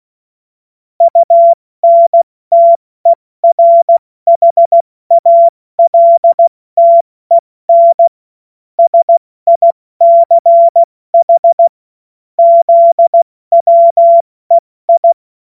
Morse code, heard as a keyed tone, about 16 wpm, 680 Hz. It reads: UNTERHALTEN SICH ZWEI
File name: Witz-28-Oktober-2025-70-F80